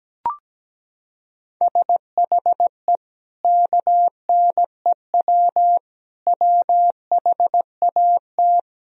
Morse code, heard as a keyed tone, about 17 wpm, 700 Hz. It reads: SHE KNEW WHAT
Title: Sentences from Top 400 Words and Abbreviations 17wpm